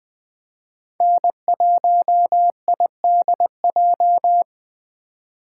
Morse code, heard as a keyed tone, about 20 wpm, 700 Hz. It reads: N1IDJ
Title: USA Call Signs - 3x Character Spacing - 20wpm